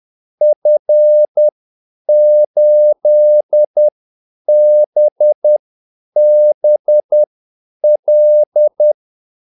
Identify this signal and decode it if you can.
Morse code, heard as a keyed tone, about 10 wpm, spaced as for 8 wpm, 600 Hz. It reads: F8BBL
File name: Callsigns-10-8wpm